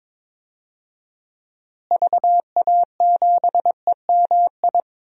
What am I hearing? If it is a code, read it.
Morse code, heard as a keyed tone, about 21 wpm, 700 Hz. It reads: VA7EMI
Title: DXCC-100 Call Signs - 6 Characters - Repeated 2x - 22wpm